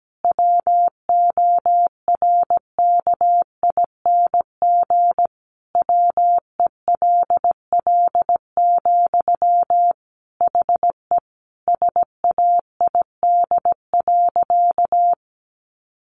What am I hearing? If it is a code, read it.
Morse code, heard as a keyed tone, about 17 wpm, 700 Hz. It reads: WORKING WELL, HE SAID.
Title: Sentences from Top 700 Words 17wpm